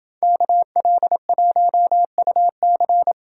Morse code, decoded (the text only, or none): KL1UC